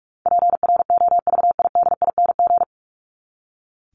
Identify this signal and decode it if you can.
Morse code, heard as a keyed tone, about 44 wpm, 700 Hz. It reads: PROVIDING
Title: ING Suffix - Top 100 Words with Base Words 45wpm